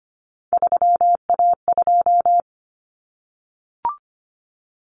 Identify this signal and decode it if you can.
Morse code, heard as a keyed tone, about 25 wpm, 700 Hz. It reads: 3A2
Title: Common Cut Numbers - 3-Digit 25wpm